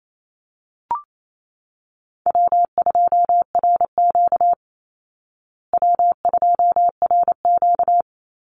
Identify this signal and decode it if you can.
Morse code, heard as a keyed tone, about 28 wpm, 700 Hz. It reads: W2RQ W2RQ
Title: Call Signs Repeated 3x - 28wpm